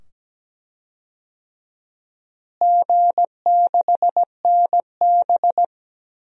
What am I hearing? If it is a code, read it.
Morse code, heard as a keyed tone, about 17 wpm, 700 Hz. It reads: G6NB
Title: DXCC-100 Call Signs - 4 Characters - Repeated 2x - 17wpm